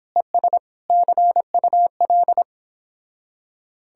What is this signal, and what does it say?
Morse code, heard as a keyed tone, about 31 wpm, 700 Hz. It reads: ES CUL